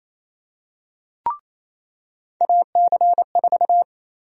Morse code, heard as a keyed tone, about 28 wpm, 700 Hz. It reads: AC4
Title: USA Call Sign Prefixes - 28wpm